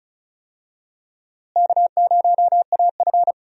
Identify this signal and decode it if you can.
Morse code, heard as a keyed tone, about 34 wpm, 700 Hz. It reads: K0AF
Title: USA Call Signs - 4 Characters - Repeated 2x - 35wpm